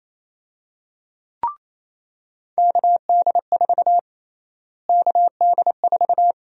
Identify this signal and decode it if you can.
Morse code, heard as a keyed tone, about 27 wpm, 700 Hz. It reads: KD4 KD4
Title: USA Call Sign Prefixes - Repeated 2x - 28wpm